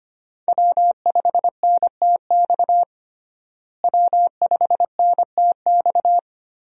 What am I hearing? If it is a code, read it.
Morse code, heard as a keyed tone, about 25 wpm, 700 Hz. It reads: W5NTX W5NTX
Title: USA Call Signs - 5 Characters - Repeated 2x - 25wpm